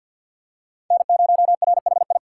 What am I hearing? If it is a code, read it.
Morse code, heard as a keyed tone, about 47 wpm, 700 Hz. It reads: N0RHI
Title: USA Call Signs - 5 Characters - 3x Character Spacing - 50wpm